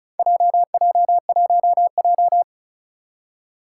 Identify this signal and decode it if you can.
Morse code, heard as a keyed tone, about 36 wpm, 700 Hz. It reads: JJ1J
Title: Four Challenging Letter-Number Combinations 35wpm